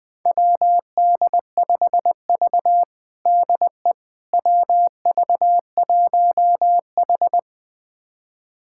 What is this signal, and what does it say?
Morse code, heard as a keyed tone, about 20 wpm, 700 Hz. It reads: WD5V DE WV1H
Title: USA Callsign DE Callsign - 4 Characters - 20wpm